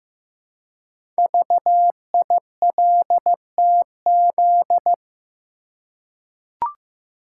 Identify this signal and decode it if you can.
Morse code, heard as a keyed tone, about 15 wpm, 700 Hz. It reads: VILTZ
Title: Code Groups - 5 Letters - 15wpm